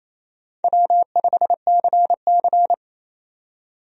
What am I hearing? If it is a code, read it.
Morse code, heard as a keyed tone, about 27 wpm, 700 Hz. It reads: W5CC
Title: USA Call Signs - 4 Characters - 28wpm